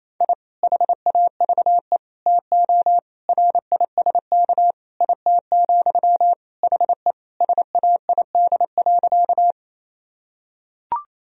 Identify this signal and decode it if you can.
Morse code, heard as a keyed tone, about 28 wpm, 700 Hz. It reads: I HAVE TO RISK IT, HE SAID.
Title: Sentences from Top 1000 Words - Review Part 4 - 28wpm